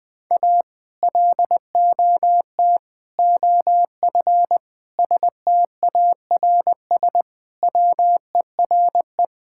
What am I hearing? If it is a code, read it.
Morse code, heard as a keyed tone, about 21 wpm, 700 Hz. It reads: A LOT OF STARS WERE
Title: Sentences with 2-Word Phrases - 20wpm